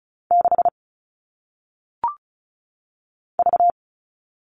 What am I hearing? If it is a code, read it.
Morse code, heard as a keyed tone, about 36 wpm, 700 Hz. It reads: 6 V
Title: All Cut Numbers - 1-Digit 35wpm